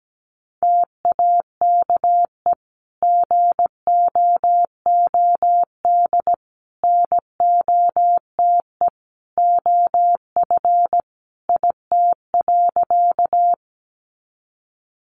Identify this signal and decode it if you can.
Morse code, heard as a keyed tone, about 17 wpm, 700 Hz. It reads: TAKE GOOD NOTE OF IT.
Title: Sentences from Top 500 Words 17wpm